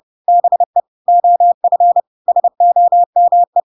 Morse code, heard as a keyed tone, about 30 wpm, 700 Hz. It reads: BE OF SOME
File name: Pinocchio / Pinocchio ch7 30 30 ik1hge